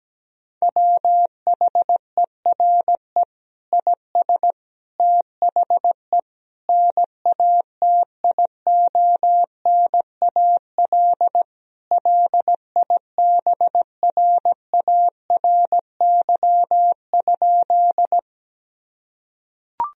Morse code, heard as a keyed tone, about 17 wpm, 700 Hz. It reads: WHERE IS THE NATIONAL LIBRARY?
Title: Sentences from Top 1000 Words - Review Part 4 - 17wpm